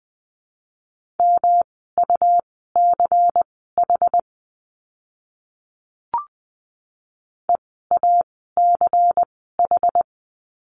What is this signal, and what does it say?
Morse code, heard as a keyed tone, about 20 wpm, spaced as for 15 wpm, 700 Hz. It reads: MUCH EACH
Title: 192 Lesson 31f - Letter v - Words 20wpm